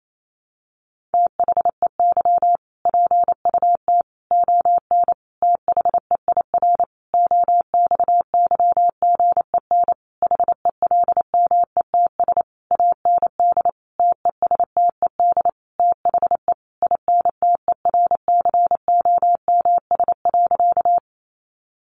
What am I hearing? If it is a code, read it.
Morse code, heard as a keyed tone, about 28 wpm, 700 Hz. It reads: THEY PUT ON THEIR OXYGEN HELMETS AND TESTED THE INTERCOMS.
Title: A Sound Of Thunder 28wpm